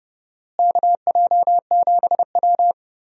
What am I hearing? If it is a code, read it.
Morse code, heard as a keyed tone, about 30 wpm, 700 Hz. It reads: KJ7W
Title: USA Call Signs - 4 Characters - Repeated 2x - 30wpm